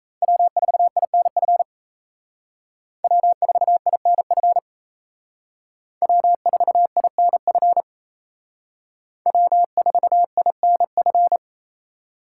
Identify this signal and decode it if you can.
Morse code, heard as a keyed tone, about 35 wpm, 700 Hz. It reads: W4INF W4INF W4INF W4INF